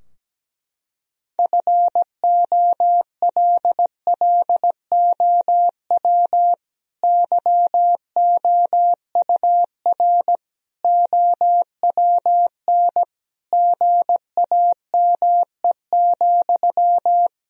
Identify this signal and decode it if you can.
Morse code, heard as a keyed tone, about 17 wpm, 700 Hz. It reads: FOLLOW YOUR OWN GAME,
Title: Sentences from Top 1000 Words - Review Part 3 - 17wpm